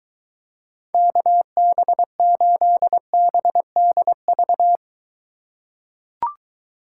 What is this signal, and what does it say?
Morse code, heard as a keyed tone, about 23 wpm, 700 Hz. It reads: KB8BDV